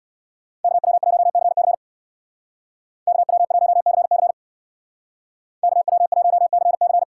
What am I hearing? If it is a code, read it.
Morse code, heard as a keyed tone, about 58 wpm, 700 Hz. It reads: KW1CC KW1CC KW1CC
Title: Speed Racing - USA 2x2 Call Signs 50wpm